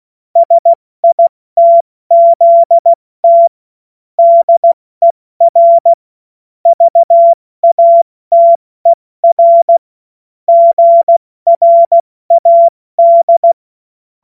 Morse code, heard as a keyed tone, about 16 wpm, 680 Hz. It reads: SITZT DER VATER GRAD